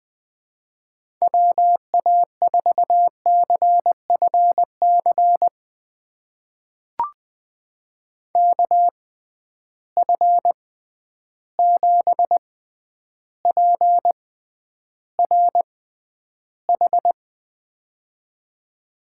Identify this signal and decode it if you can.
Morse code, heard as a keyed tone, about 20 wpm, spaced as for 10 wpm, 700 Hz. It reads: WA4CFC KF7PRH
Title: USA Call Signs - 6 Characters - 6x Character Spacing - 20wpm